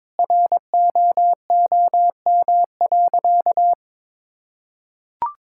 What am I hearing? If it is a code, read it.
Morse code, heard as a keyed tone, about 22 wpm, 700 Hz. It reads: ROOM.
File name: practice / 22wpm / Sentences with 4-Word Phrases - 22wpm